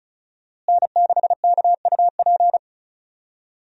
Morse code, heard as a keyed tone, about 34 wpm, 700 Hz. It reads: N6KUP
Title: USA Call Signs - 5 Characters - Repeated 3x - 35wpm